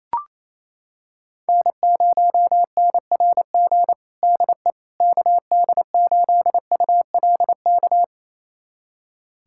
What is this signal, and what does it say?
Morse code, heard as a keyed tone, about 28 wpm, 700 Hz. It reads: N0NRG DE KD8ULK
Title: USA Callsign DE Callsign 28wpm